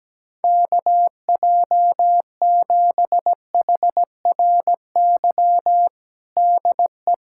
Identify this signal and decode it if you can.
Morse code, heard as a keyed tone, about 17 wpm, 700 Hz. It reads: KJ7HRY DE
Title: USA Callsign DE Callsign 17wpm